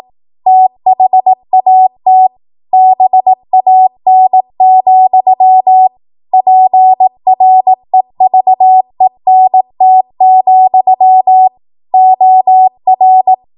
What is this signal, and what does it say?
Morse code, heard as a keyed tone, about 18 wpm, 750 Hz. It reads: THAT BAN, PREVENT, OR
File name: Morse / 18WPM